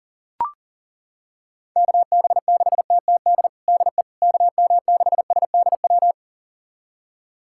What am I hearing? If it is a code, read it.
Morse code, heard as a keyed tone, about 40 wpm, 700 Hz. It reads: KB6TTD DE KM6SDW
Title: USA Callsign DE Callsign - 6 Characters - 40wpm